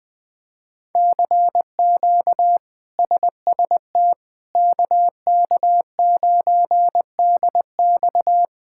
Morse code, heard as a keyed tone, about 20 wpm, 700 Hz. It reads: CQ SST KK9DX
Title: K1USN SST QSO Practice - 20wpm